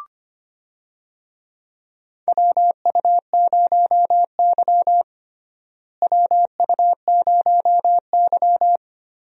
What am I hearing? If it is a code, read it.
Morse code, heard as a keyed tone, about 25 wpm, 700 Hz. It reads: WU0Y WU0Y
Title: USA Call Signs - 4 Characters - Repeated 2x - 25wpm